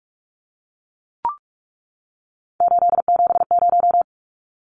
Morse code, heard as a keyed tone, about 44 wpm, 700 Hz. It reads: 870
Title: Numbers - 3-Digit 45wpm